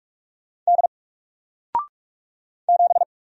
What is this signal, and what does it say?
Morse code, heard as a keyed tone, about 47 wpm, 700 Hz. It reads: D 7
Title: Single Letter-Number Alt 45wpm